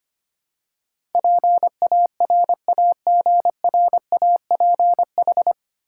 Morse code, heard as a keyed tone, about 25 wpm, 700 Hz. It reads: PARAGRAPH
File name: practice / 25wpm / 9 Letter Words - First 3 Letters Given - 25wpm